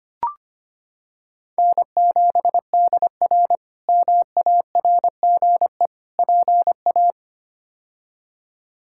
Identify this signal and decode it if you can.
Morse code, heard as a keyed tone, about 25 wpm, 700 Hz. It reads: N7DR MARGE PA
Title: K1USN SST QSO Practice - 25wpm